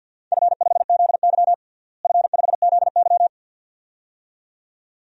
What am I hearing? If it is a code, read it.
Morse code, heard as a keyed tone, about 46 wpm, 700 Hz. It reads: F5ZY F5ZY
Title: DXCC-100 Call Signs - 4 Characters - Repeated 2x - 50wpm